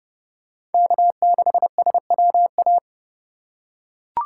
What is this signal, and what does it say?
Morse code, heard as a keyed tone, about 29 wpm, 700 Hz. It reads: K6SWA